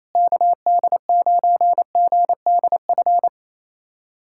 Morse code, heard as a keyed tone, about 28 wpm, 700 Hz. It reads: KD9GDF
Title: USA Callsign DE Callsign 28wpm